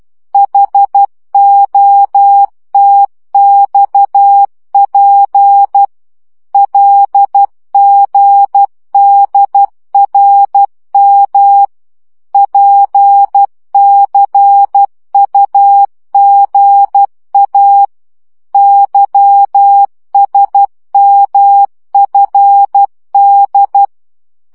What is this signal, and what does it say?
Morse code, heard as a keyed tone, about 12 wpm, 800 Hz. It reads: HOTXP LGDRM PCUGA YSMFD
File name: morze vizsga 2011-2 10